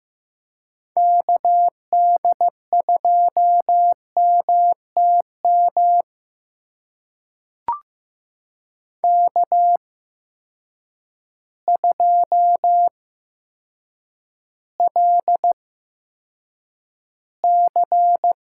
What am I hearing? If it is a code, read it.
Morse code, heard as a keyed tone, about 15 wpm, spaced as for 7 wpm, 700 Hz. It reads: KD2MTM K2LC
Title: USA Call Signs - 8x Character Spacing - 15wpm